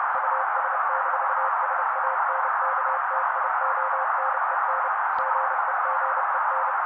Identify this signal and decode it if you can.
Morse code, heard as a keyed tone, about 30 wpm, 530 Hz. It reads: US-SANKTIONEN GEGEN
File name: daily / 29wpm noise